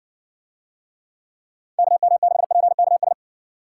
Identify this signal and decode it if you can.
Morse code, heard as a keyed tone, about 60 wpm, 700 Hz. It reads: KM6PKS